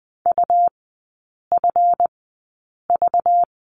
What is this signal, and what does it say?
Morse code, heard as a keyed tone, about 20 wpm, spaced as for 8 wpm, 700 Hz. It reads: UFV
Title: 191 Lesson 31e - Letter v - Three Characters 20wpm